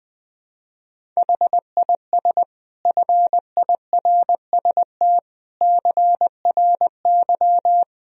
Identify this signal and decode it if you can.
Morse code, heard as a keyed tone, about 21 wpm, 700 Hz. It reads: HIS FIRST CRY
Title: Sentences from Top 500 Words and Abbreviations 20wpm